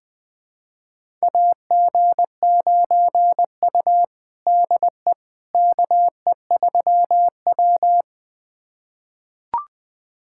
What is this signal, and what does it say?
Morse code, heard as a keyed tone, about 20 wpm, 700 Hz. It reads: AG9U DE KE3W
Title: USA Callsign DE Callsign - 4 Characters - 20wpm